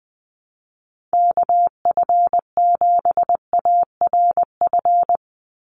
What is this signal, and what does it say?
Morse code, heard as a keyed tone, about 20 wpm, 700 Hz. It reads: KF7ARF
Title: USA Call Signs 20wpm